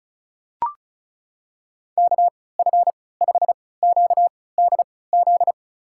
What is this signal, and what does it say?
Morse code, heard as a keyed tone, about 36 wpm, spaced as for 20 wpm, 700 Hz. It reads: KF5QDZ
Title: USA Call Signs - 3x Character Spacing - 35wpm